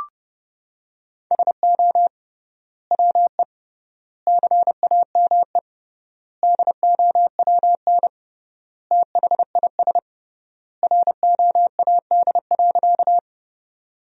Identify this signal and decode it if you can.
Morse code, heard as a keyed tone, about 30 wpm, 700 Hz. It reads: SO WE CAME DOWN THIS ROAD.
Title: 3x Word-Spacing - Sentences from Top 400 Words - 30wpm